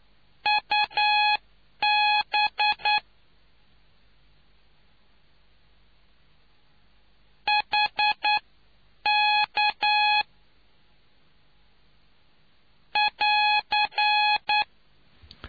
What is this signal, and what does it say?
Morse code, heard as a keyed tone, about 9 wpm, spaced as for 8 wpm, 840 Hz. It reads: UB HK <AR>